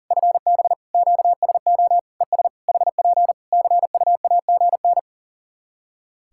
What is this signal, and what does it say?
Morse code, heard as a keyed tone, about 40 wpm, 700 Hz. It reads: FB QSO ES HP CUAGN
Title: Practice QSO - Exchange 4 - Ending - 40wpm